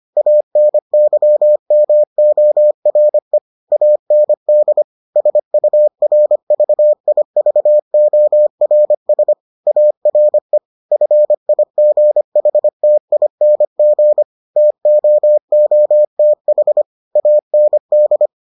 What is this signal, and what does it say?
Morse code, heard as a keyed tone, about 25 wpm, 600 Hz. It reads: ANYMORE AND SURVIVORS ARE FIGHTING TOOTH AND